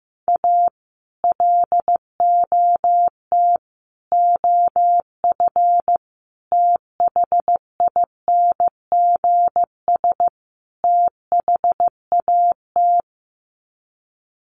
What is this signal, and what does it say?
Morse code, heard as a keyed tone, about 15 wpm, 700 Hz. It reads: A LOT OF THINGS THAT
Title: Sets of 5 Words 15wpm